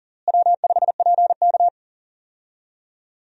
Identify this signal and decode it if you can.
Morse code, heard as a keyed tone, about 40 wpm, 700 Hz. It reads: W5PK
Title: USA Call Signs - 4 Characters - Repeated 3x - 40wpm